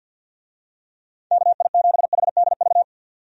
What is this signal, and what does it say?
Morse code, heard as a keyed tone, about 51 wpm, 700 Hz. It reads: KI7HDV
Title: USA Call Signs - 6 Characters - 50wpm